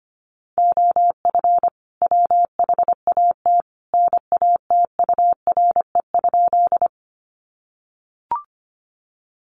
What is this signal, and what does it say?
Morse code, heard as a keyed tone, about 25 wpm, 700 Hz. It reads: OF WHAT NATURE?
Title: Sentences from Top 600 Words 25wpm